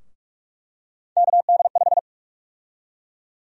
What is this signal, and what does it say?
Morse code, heard as a keyed tone, about 46 wpm, 700 Hz. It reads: KD5
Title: USA Call Sign Prefixes - Repeated 2x - 45wpm